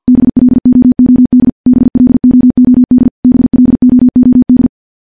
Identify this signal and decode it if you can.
Morse code, heard as a keyed tone, about 50 wpm, 250 Hz. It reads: BLOOD BLOOD BLOOD
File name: morse / morse-training-50WPM